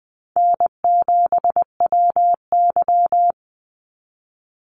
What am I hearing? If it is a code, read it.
Morse code, heard as a keyed tone, about 20 wpm, 700 Hz. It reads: N7WY
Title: Easy Call Signs Repeated 3x - 20wpm